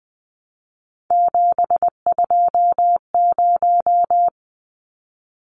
Morse code, heard as a keyed tone, about 20 wpm, 700 Hz. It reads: 720
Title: Numbers - 3-Digit 20wpm